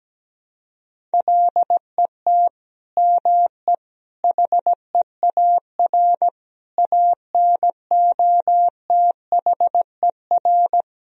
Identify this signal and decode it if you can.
Morse code, heard as a keyed tone, about 17 wpm, 700 Hz. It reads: LET ME HEAR ANOTHER
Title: Sentences from Top 1000 Words - Review Part 2 - 17wpm